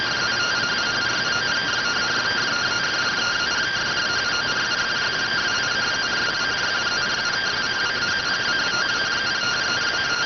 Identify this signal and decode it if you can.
Two-tone teleprinter signal, about 170 Hz shift.